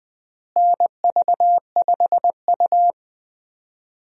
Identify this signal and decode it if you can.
Morse code, heard as a keyed tone, about 21 wpm, 700 Hz. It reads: NV5U
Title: USA Call Signs - 4 Characters - Repeated 3x - 20wpm